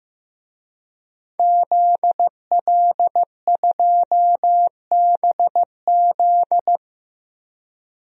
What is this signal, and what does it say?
Morse code, heard as a keyed tone, about 15 wpm, 700 Hz. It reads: ZL2BZ
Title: DXCC-100 Call Signs - 5 Characters - Repeated 3x - 15wpm